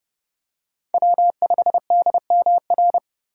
Morse code, heard as a keyed tone, about 32 wpm, 700 Hz. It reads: W5DMR
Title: USA Call Signs - 5 Characters - 3x Character Spacing - 30wpm